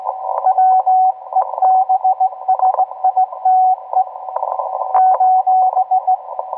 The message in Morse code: WEBSITE :